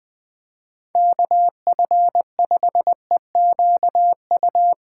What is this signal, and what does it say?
Morse code, heard as a keyed tone, about 20 wpm, 700 Hz. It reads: KF5EQU
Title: Speed Racing - USA 2x3 Call Signs 17wpm